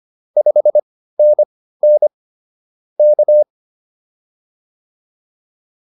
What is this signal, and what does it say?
Morse code, heard as a keyed tone, about 25 wpm, spaced as for 15 wpm, 590 Hz. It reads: 5NN K